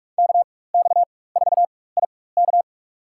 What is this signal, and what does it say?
Morse code, heard as a keyed tone, about 43 wpm, spaced as for 21 wpm, 700 Hz. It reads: KX4IK
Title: USA Call Signs - 5 Characters - 4x Character Spacing - 45wpm